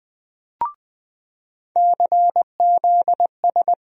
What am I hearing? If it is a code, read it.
Morse code, heard as a keyed tone, about 20 wpm, 700 Hz. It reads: CZS
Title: USA Call Sign Postfixes - 3 Letters 20wpm